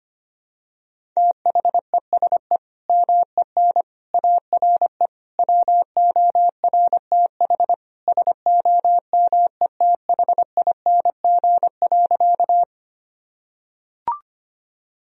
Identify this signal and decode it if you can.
Morse code, heard as a keyed tone, about 25 wpm, 700 Hz. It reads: THESE MEN ARE WORTH SOMETHING.
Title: Sentences from Top 1000 Words - Review Part 1 - 25wpm